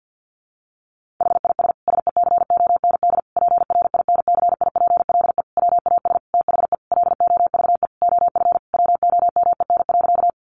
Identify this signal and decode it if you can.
Morse code, heard as a keyed tone, about 49 wpm, 700 Hz. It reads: HIS SECOND PRINCIPLE WAS THE LOVE OF WOMEN.
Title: Sentences from Top 1400 Words 50wpm